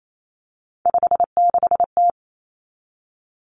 Morse code, heard as a keyed tone, about 28 wpm, 700 Hz. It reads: 56T
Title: Three Challenging Letter-Number Combinations 28wpm